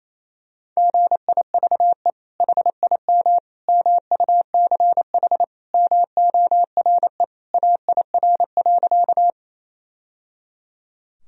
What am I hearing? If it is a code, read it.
Morse code, heard as a keyed tone, about 28 wpm, 700 Hz. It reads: GIVE HIM MUCH MORE AIR.
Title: Sentences from Top 1000 Words - Review Part 1 - 28wpm